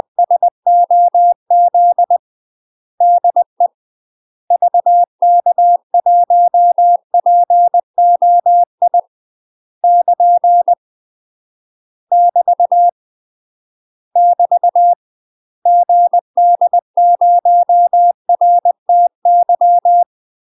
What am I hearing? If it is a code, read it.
Morse code, heard as a keyed tone, about 20 wpm, 700 Hz. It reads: SOZ DE VK1POI <KN> <BT> <BT> GD0RTY
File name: UK MASTER QSOs TRAINING / 20WPM / UK MASTER QSOs 0066